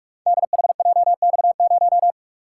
Morse code, heard as a keyed tone, about 43 wpm, 700 Hz. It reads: DHJX0
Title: Code Groups - 5 Letter:Numbers - 45wpm